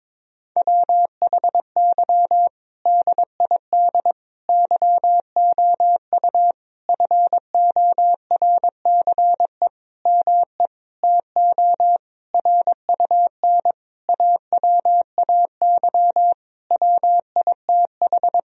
Morse code, heard as a keyed tone, about 22 wpm, 700 Hz. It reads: WHY DID YOU FORCE ME TO RUN AWAY WITH
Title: Sentences from Top 1000 Words - Review Part 3 - 22wpm